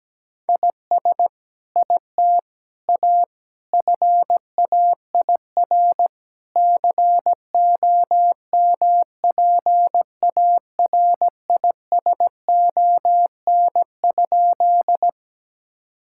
Morse code, heard as a keyed tone, about 17 wpm, 700 Hz. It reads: IS IT A FAIR COMPARISON?
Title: Sentences from Top 1600 Words 17wpm